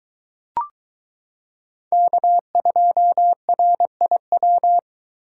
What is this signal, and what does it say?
Morse code, heard as a keyed tone, about 23 wpm, 700 Hz. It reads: K2RIW